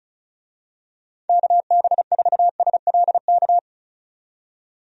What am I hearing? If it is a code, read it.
Morse code, heard as a keyed tone, about 34 wpm, 700 Hz. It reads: KB4SLK
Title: USA Call Signs - 6 Characters - 35wpm